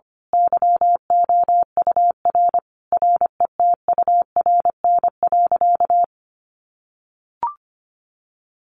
Morse code, heard as a keyed tone, about 25 wpm, 700 Hz. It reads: YOUR RETURN.
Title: A Sound Of Thunder 25wpm